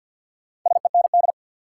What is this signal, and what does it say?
Morse code, heard as a keyed tone, about 53 wpm, 700 Hz. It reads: SEND
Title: Sentences from Top 1500 Words - New Word Drill - 50wpm